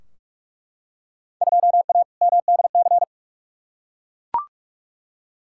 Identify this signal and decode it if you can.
Morse code, heard as a keyed tone, about 43 wpm, 700 Hz. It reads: 2A MDC
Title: Field Day Exchanges - 45wpm